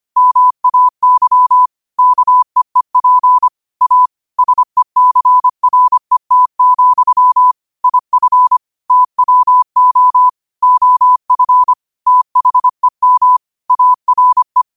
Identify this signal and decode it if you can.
Morse code, heard as a keyed tone, about 25 wpm, 1000 Hz. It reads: MAY KEEP A SECRET, IF TWO OF THEM ARE
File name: 25wpm